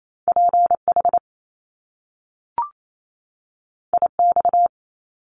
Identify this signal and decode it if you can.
Morse code, heard as a keyed tone, about 28 wpm, 700 Hz. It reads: PH IX
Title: Top Two Letter Combinations in Words 28wpm